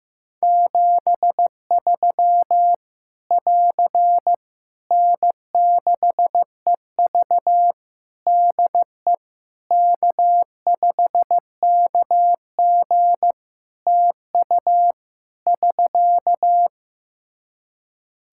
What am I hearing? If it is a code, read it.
Morse code, heard as a keyed tone, about 15 wpm, 700 Hz. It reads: 73 <AR> N6EV DE K5KG TU <SK>
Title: K1USN SST QSO Practice 15wpm